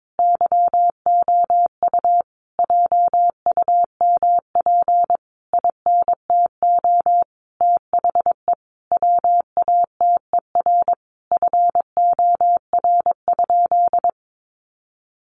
Morse code, heard as a keyed tone, about 22 wpm, 700 Hz. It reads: YOU JUMP INTO THE WATER FOR?
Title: Sentences from Top 1300 Words 22wpm